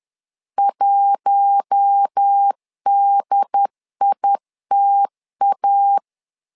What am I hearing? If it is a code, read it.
Morse code, heard as a keyed tone, about 10 wpm, 800 Hz. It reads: 1DITA
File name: LEZIONI / 4 Q1 10wpm 2448